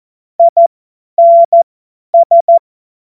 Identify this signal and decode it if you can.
Morse code, heard as a keyed tone, about 14 wpm, spaced as for 10 wpm, 680 Hz. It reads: INS
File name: Morsekurs / Witze / Witz 02. Mai 2023 50 F70